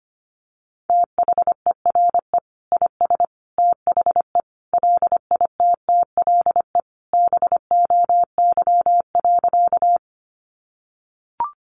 Morse code, heard as a keyed tone, about 25 wpm, 700 Hz. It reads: THERE IS THE LITTLE BOY.
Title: Sentences from Top 200 Words 25wpm